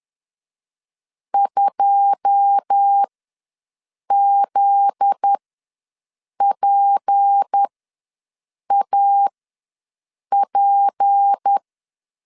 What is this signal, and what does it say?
Morse code, heard as a keyed tone, about 10 wpm, spaced as for 6 wpm, 800 Hz. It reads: 2ZPAP
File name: LEZIONI / 10 SIGNS / 10D COMA ws 10wpm 4314